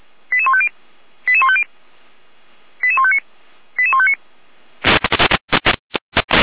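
Five-tone selective call (ZVEI), heard as quick runs of five tones, tones of about 70 ms. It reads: 88150 88150 88159 88159
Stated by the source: ZVEI five-tone selective call received on SDR.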